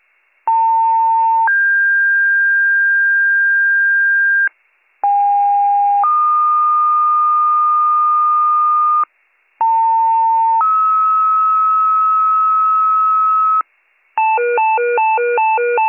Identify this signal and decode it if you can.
Two-tone page, heard as a one-second tone, then a three-second tone, about 900 then 1600 Hz, about 810 then 1220 Hz, about 900 then 1290 Hz. It is heard from 0.5 to 13.6 s.